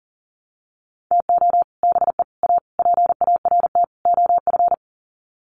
Morse code, heard as a keyed tone, about 40 wpm, 700 Hz. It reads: TO BE A PART OF
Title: Sets of 5 Words 40wpm